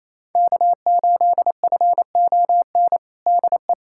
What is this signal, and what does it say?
Morse code, heard as a keyed tone, about 28 wpm, 700 Hz. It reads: K8FON DE
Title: USA Callsign DE Callsign - 5 Characters - 28wpm